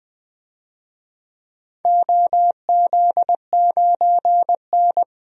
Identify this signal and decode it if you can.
Morse code, heard as a keyed tone, about 21 wpm, 700 Hz. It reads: OZ9N
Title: DXCC-100 Call Signs - 4 Characters - 20wpm